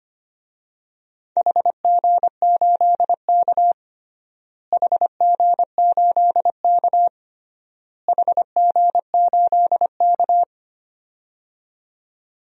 Morse code, heard as a keyed tone, about 25 wpm, 700 Hz. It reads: HG8K HG8K HG8K
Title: DXCC-100 Call Signs - 4 Characters - Repeated 3x - 25wpm